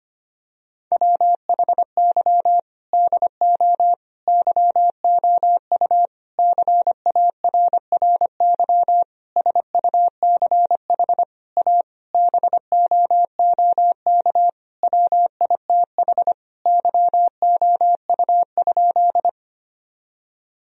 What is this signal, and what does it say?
Morse code, heard as a keyed tone, about 25 wpm, 700 Hz. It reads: WHY DO YOU CARRY SUCH A BOOK WITH YOU?
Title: Sentences from Top 1000 Words - Review Part 1 - 25wpm